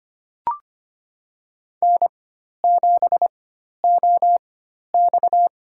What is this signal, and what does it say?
Morse code, heard as a keyed tone, about 25 wpm, spaced as for 12 wpm, 700 Hz. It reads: N7OX
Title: USA Call Signs - 4x Character Spacing - 25wpm